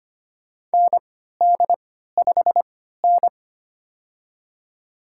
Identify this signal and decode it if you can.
Morse code, heard as a keyed tone, about 25 wpm, spaced as for 14 wpm, 700 Hz. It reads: ND5N
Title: USA Call Signs - 4 Characters - 3x Character Spacing - 25wpm